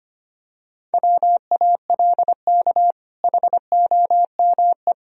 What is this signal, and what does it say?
Morse code, heard as a keyed tone, about 25 wpm, 700 Hz. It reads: WALK HOME
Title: Sentences from Top 1000 Words - Review Part 1 - 25wpm